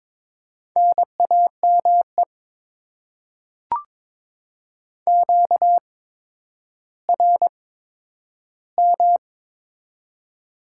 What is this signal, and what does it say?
Morse code, heard as a keyed tone, about 22 wpm, spaced as for 11 wpm, 700 Hz. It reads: NAME QRM
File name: Top 90 QSO Elements - 8x Character Spacing - 22wpm